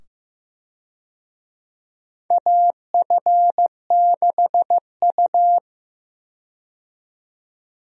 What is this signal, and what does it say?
Morse code, heard as a keyed tone, about 15 wpm, 700 Hz. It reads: AF6U
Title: USA Call Signs - 4 Characters - 15wpm